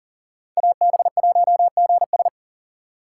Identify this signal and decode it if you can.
Morse code, heard as a keyed tone, about 40 wpm, 700 Hz. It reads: AB1GS
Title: USA Call Signs - 5 Characters - 6x Character Spacing - 40wpm